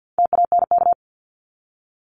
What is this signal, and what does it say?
Morse code, heard as a keyed tone, about 48 wpm, 700 Hz. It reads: TANK
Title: Top 1751-2000 Words 50wpm